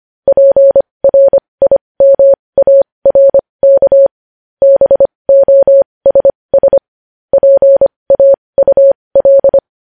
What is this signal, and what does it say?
Morse code, heard as a keyed tone, about 25 wpm, spaced as for 20 wpm, 550 Hz. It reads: PRIMARK BOSS PAUL
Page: news headlines in morse code at 20 wpm